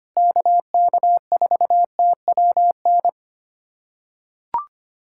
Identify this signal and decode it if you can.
Morse code, heard as a keyed tone, about 25 wpm, 700 Hz. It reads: KK4TWN
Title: USA Callsign DE Callsign - 6 Characters - 25wpm